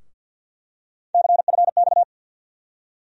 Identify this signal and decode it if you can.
Morse code, heard as a keyed tone, about 53 wpm, 700 Hz. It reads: CFX
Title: USA Call Sign Postfixes - 3 Letters 50wpm